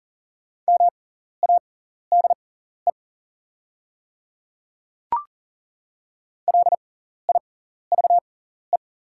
Morse code, heard as a keyed tone, about 39 wpm, spaced as for 14 wpm, 690 Hz. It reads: MADE LIVE
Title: 6x Character Spacing - Top 200 Words - 40wpm